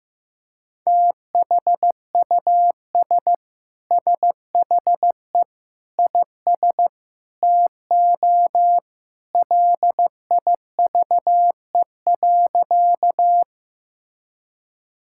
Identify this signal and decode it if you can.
Morse code, heard as a keyed tone, about 15 wpm, 700 Hz. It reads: THUS SHE IS TO LIVE.
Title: Sentences from Top 1000 Words - Review Part 5 - 15wpm